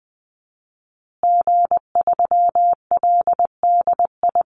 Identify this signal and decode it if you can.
Morse code, heard as a keyed tone, about 20 wpm, 700 Hz. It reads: G3LDI
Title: Call Signs Repeated 2x - 20wpm